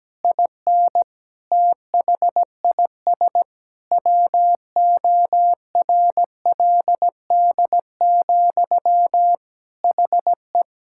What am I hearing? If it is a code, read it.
Morse code, heard as a keyed tone, about 17 wpm, 700 Hz. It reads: IN THIS WORLD, HE
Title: Sentences from Top 1600 Words 17wpm